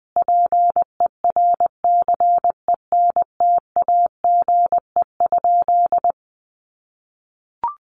Morse code, heard as a keyed tone, about 20 wpm, 700 Hz. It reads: PERCENTAGE?
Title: Sentences from Top 1300 Words 20wpm